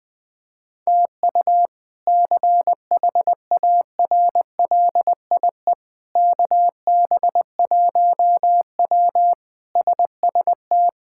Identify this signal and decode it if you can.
Morse code, heard as a keyed tone, about 21 wpm, 700 Hz. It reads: TU CHARLIE KB1W SST
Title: K1USN SST QSO Practice - 20wpm